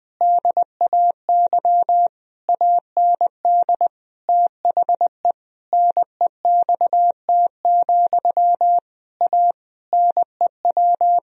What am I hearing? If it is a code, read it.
Morse code, heard as a keyed tone, about 20 wpm, 700 Hz. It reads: DAY AND THE NEXT, A NEW
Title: Sentences from Top 1000 Words - Review Part 1 - 20wpm